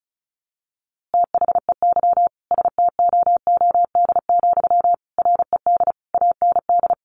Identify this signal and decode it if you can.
Morse code, heard as a keyed tone, about 34 wpm, 700 Hz. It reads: THEY STOOD, RED AND
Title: A Sound Of Thunder 35wpm